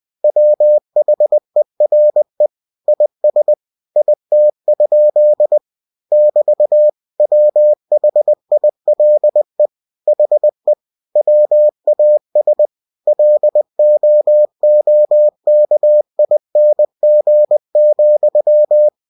Morse code, heard as a keyed tone, about 20 wpm, 600 Hz. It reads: WHERE IS IT? <BT> WHILE HE WAS LOOKING,
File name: cw-books / winnie-the-pooh / winnie-20wpm-page-0107